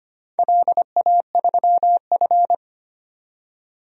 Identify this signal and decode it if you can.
Morse code, heard as a keyed tone, about 25 wpm, 700 Hz. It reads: LA3F
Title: DXCC-100 Call Signs - 4 Characters - 25wpm